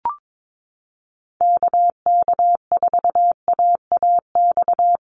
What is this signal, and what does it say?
Morse code, heard as a keyed tone, about 22 wpm, 700 Hz. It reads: KK4AAX
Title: USA Call Signs Repeated 3x 22wpm